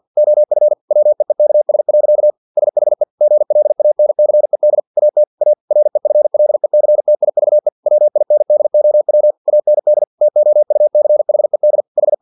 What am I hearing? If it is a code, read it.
Morse code, heard as a keyed tone, about 50 wpm, 600 Hz. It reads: OF WEEKS. SHE GLANCED AT A REFLECTIVE WINDOW AND TOUCHED H
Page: Never 49wpm